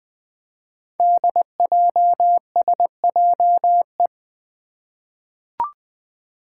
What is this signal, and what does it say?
Morse code, heard as a keyed tone, about 20 wpm, 700 Hz. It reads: DJSJE
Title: Five Challenging Letter-Number Combinations 20wpm